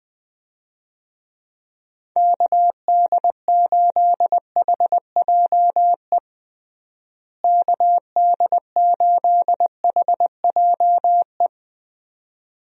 Morse code, heard as a keyed tone, about 20 wpm, 700 Hz. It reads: KD8HJE KD8HJE
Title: USA Call Signs - 6 Characters - Repeated 2x - 20wpm